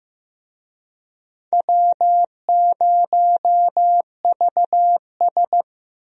Morse code, heard as a keyed tone, about 15 wpm, 700 Hz. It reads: W0VS
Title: USA Call Signs - 4 Characters - Repeated 2x - 15wpm